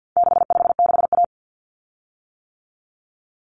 Morse code, heard as a keyed tone, about 50 wpm, 700 Hz. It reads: 656A
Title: Common Cut Numbers - 4-Digit 50wpm